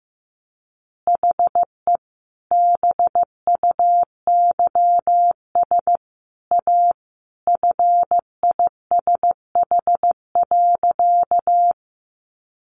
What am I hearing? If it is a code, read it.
Morse code, heard as a keyed tone, about 14 wpm, 700 Hz. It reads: HE BUYS A FISH.
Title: Additional Easy Phrases 15wpm